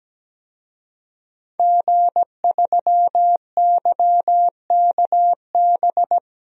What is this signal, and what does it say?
Morse code, heard as a keyed tone, about 17 wpm, 700 Hz. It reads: G3YKB
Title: DXCC-100 Call Signs - 5 Characters - 17wpm